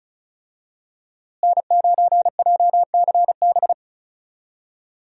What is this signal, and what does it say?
Morse code, heard as a keyed tone, about 36 wpm, 700 Hz. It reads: N9JCB
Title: USA Call Signs - 5 Characters - Repeated 3x - 35wpm